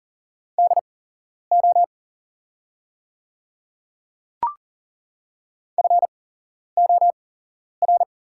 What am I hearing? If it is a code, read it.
Morse code, heard as a keyed tone, about 40 wpm, spaced as for 11 wpm, 690 Hz. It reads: DO FOR
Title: 8x Character Spacing - Top 100 Words - 40wpm